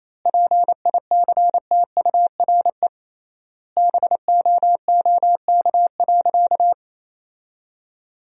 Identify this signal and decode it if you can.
Morse code, heard as a keyed tone, about 28 wpm, 700 Hz. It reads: PICTURE BOOK.
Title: 3x Word-Spacing - Sentences from Top 500 Words - 28wpm